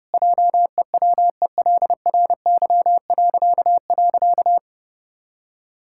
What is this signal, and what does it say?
Morse code, heard as a keyed tone, about 30 wpm, 700 Hz. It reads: JEWELRY..
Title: Pangram Challenge v2 - 30wpm